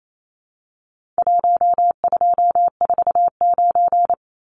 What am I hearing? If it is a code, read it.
Morse code, heard as a keyed tone, about 28 wpm, 700 Hz. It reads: 1249
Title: Numbers - 4-Digit 28wpm